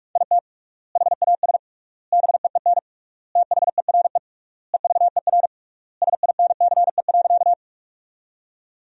Morse code, heard as a keyed tone, about 41 wpm, 700 Hz. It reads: IT HAS BEEN THERE EVER SINCE.
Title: 3x Word-Spacing - Sentences from Top 400 Words - 45wpm